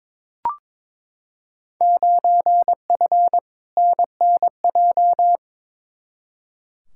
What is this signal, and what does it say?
Morse code, heard as a keyed tone, about 22 wpm, 700 Hz. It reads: 9F NNJ
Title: Field Day Exchanges Encore - 22wpm